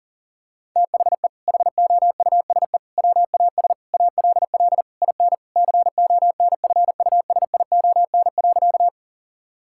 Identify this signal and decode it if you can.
Morse code, heard as a keyed tone, about 40 wpm, 700 Hz. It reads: THE HOUSE WAS ALL IN CONFUSION.
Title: Sentences from Top 1600 Words 40wpm